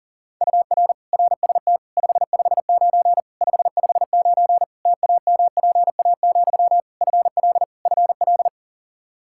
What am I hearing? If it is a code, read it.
Morse code, heard as a keyed tone, about 40 wpm, 700 Hz. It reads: UR RST 559 559 TAMPA, FL FL
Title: Practice QSO - Exchange 1 - Beginning - 40wpm